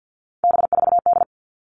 Morse code, heard as a keyed tone, about 48 wpm, 700 Hz. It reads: B4D
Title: All Cut Numbers - 3-Digit 50wpm